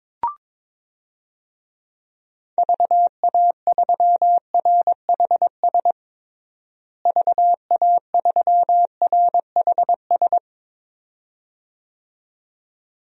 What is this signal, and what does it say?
Morse code, heard as a keyed tone, about 22 wpm, 700 Hz. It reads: VA3RHS VA3RHS
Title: DXCC-100 Call Signs - 6 Characters - Repeated 2x - 22wpm